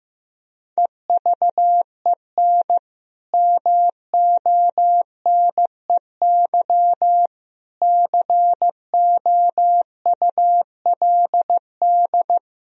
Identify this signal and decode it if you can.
Morse code, heard as a keyed tone, about 15 wpm, 700 Hz. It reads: EVEN MONEY COULD
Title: Sentences from Top 1000 Words - Review Part 4 - 15wpm